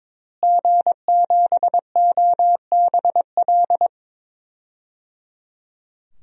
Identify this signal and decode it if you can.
Morse code, heard as a keyed tone, about 22 wpm, 700 Hz. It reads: G7OBL
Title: DXCC-100 Call Signs - 5 Characters - Repeated 3x - 22wpm